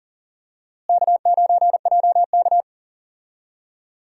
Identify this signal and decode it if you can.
Morse code, heard as a keyed tone, about 40 wpm, 700 Hz. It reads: K9JK
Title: USA Call Signs - 4 Characters - Repeated 2x - 40wpm